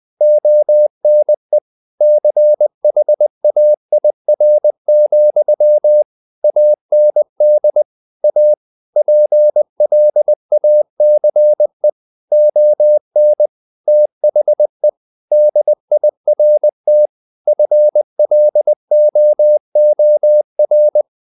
Morse code, heard as a keyed tone, about 20 wpm, 600 Hz. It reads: ONE CHAIR, AND A PLACE ON THE DIRT FLOOR
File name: the old man and the sea / the old man and the sea part 1 20wpm